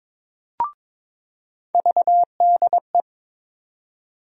Morse code, heard as a keyed tone, about 21 wpm, 700 Hz. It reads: VDE
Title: USA Call Sign Postfixes - 3 Letters 22wpm